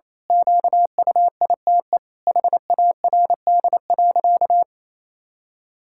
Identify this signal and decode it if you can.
Morse code, heard as a keyed tone, about 28 wpm, 700 Hz. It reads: QUITE HARD.
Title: Pangram Challenge v2 - 28wpm